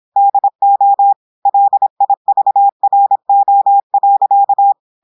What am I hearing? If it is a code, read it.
Morse code, heard as a keyed tone, about 26 wpm, 800 Hz. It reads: DO LIVRO.
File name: audios / portugais / portugais texte en clair 1 vitesse 26 mots minutes